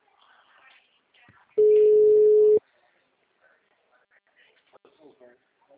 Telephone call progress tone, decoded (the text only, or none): ringback tone